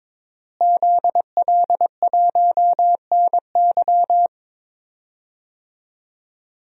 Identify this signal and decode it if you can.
Morse code, heard as a keyed tone, about 22 wpm, 700 Hz. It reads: ZL1NY
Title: DXCC-100 Call Signs - 5 Characters - Repeated 3x - 22wpm